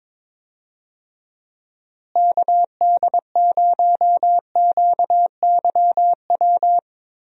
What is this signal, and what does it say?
Morse code, heard as a keyed tone, about 22 wpm, 700 Hz. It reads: KD0QYW
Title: USA Call Signs - 6 Characters - Repeated 2x - 22wpm